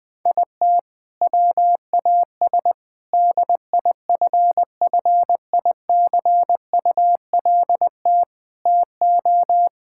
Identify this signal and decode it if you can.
Morse code, heard as a keyed tone, about 20 wpm, 700 Hz. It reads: IT WAS DIFFICULT TO
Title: Sentences from Top 1000 Words - Review Part 5 - 20wpm